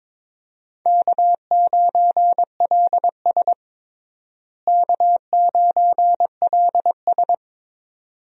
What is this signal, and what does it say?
Morse code, heard as a keyed tone, about 22 wpm, 700 Hz. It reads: K9LS K9LS
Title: USA Call Signs - 4 Characters - Repeated 2x - 22wpm